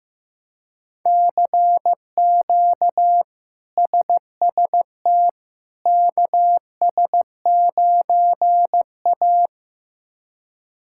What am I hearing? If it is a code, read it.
Morse code, heard as a keyed tone, about 16 wpm, 700 Hz. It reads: CQ SST KS9A
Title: K1USN SST QSO Practice - 15wpm